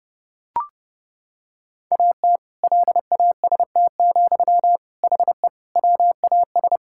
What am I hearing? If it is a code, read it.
Morse code, heard as a keyed tone, about 32 wpm, 700 Hz. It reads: AT LAST, HE WAS
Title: Sentences from Top 1000 Words - Review Part 6 - 30wpm